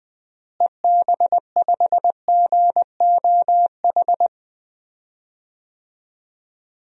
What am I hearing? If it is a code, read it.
Morse code, heard as a keyed tone, about 20 wpm, 700 Hz. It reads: EB5GOH
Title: DXCC-100 Call Signs - 6 Characters - Repeated 2x - 20wpm